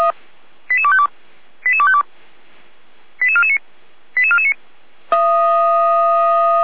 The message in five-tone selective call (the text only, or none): 88252 88252 88449 88449